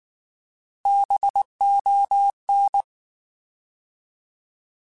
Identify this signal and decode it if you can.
Morse code, heard as a keyed tone, about 19 wpm, 800 Hz. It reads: BON
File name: Alphabetical / Alphabetical mp3s / Bs @18